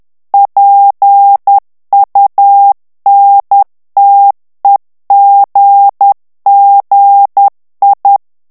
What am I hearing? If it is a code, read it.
Morse code, heard as a keyed tone, about 10 wpm, 800 Hz. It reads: PUNTEGGI
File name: ITALIEN texte clair 2 vitesse 10 mots minute